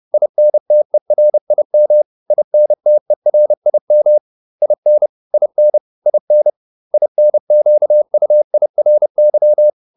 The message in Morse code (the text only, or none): INTERIM INTERIM IN IN IN INQUIRY